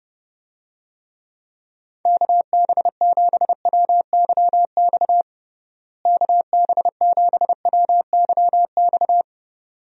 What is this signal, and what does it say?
Morse code, heard as a keyed tone, about 32 wpm, 700 Hz. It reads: KB7WYX KB7WYX
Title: USA Call Signs - 6 Characters - Repeated 2x - 30wpm